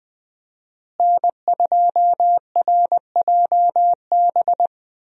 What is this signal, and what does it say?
Morse code, heard as a keyed tone, about 20 wpm, 700 Hz. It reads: N2RJB
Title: USA Call Signs - 5 Characters - Repeated 2x - 20wpm